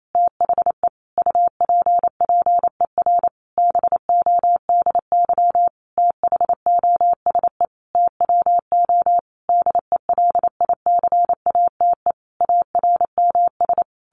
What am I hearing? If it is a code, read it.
Morse code, heard as a keyed tone, about 28 wpm, 700 Hz. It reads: THE UPPER BODY THOSE TWO DELICATE ARMS
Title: A Sound Of Thunder 28wpm